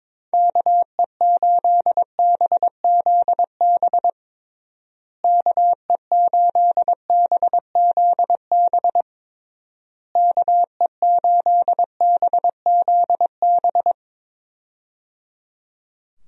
Morse code, heard as a keyed tone, about 22 wpm, 700 Hz. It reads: KE8BZB KE8BZB KE8BZB
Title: USA Call Signs - 6 Characters - Repeated 3x - 22wpm